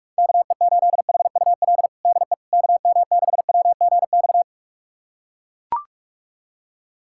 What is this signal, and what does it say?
Morse code, heard as a keyed tone, about 42 wpm, 700 Hz. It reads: KE8HUL DE KM6WGX
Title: USA Callsign DE Callsign 45wpm